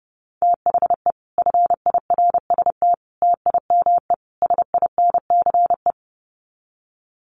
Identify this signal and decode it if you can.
Morse code, heard as a keyed tone, about 30 wpm, 700 Hz. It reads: THE FIRST TIME SINCE
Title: Sets of 4 Words 30wpm